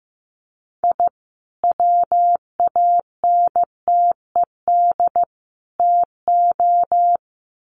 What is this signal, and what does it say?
Morse code, heard as a keyed tone, about 15 wpm, 700 Hz. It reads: I WANTED TO
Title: Sets of 3 Words 15wpm